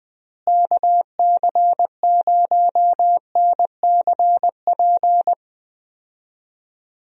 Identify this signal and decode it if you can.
Morse code, heard as a keyed tone, about 21 wpm, 700 Hz. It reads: KC0NCP
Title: USA Call Signs - 6 Characters - Repeated 3x - 20wpm